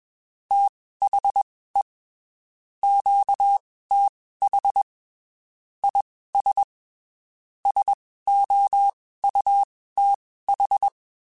Morse code, heard as a keyed tone, about 21 wpm, spaced as for 15 wpm, 800 Hz. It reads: THE QTH IS SOUTH
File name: (spaced) / QSOs Spaced-mp3s / WA7QOX 2 @20 sp